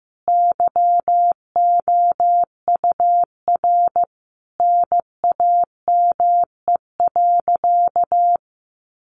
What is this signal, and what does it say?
Morse code, heard as a keyed tone, about 15 wpm, 700 Hz. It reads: YOUR NAME.
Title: Sentences from Top 200 Words 15wpm